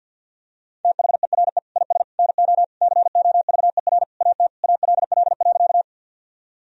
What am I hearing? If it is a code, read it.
Morse code, heard as a keyed tone, about 50 wpm, 700 Hz. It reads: THERE IS NO COVER AT ALL.
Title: Sentences from Top 1000 Words - Review Part 1 - 50wpm